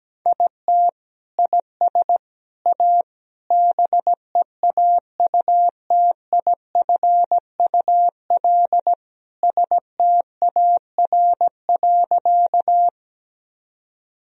Morse code, heard as a keyed tone, about 17 wpm, 700 Hz. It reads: IT IS A BEAUTIFUL STAR.
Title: Sentences from Top 1000 Words - Review Part 2 - 17wpm